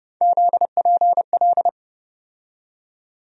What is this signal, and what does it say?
Morse code, heard as a keyed tone, about 30 wpm, 700 Hz. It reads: ZPL
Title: USA Call Sign Postfixes - 3 Letters 30wpm